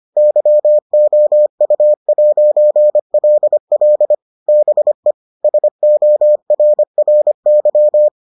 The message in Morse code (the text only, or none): YOU'LL BE SORRY